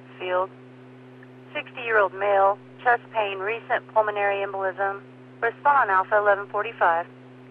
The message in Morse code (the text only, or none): none